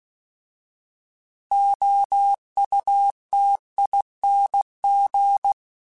Morse code, heard as a keyed tone, about 16 wpm, 800 Hz. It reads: OUTING